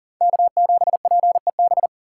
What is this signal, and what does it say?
Morse code, heard as a keyed tone, about 50 wpm, 700 Hz. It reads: K7PEB
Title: Speed Racing - USA Call Signs 40wpm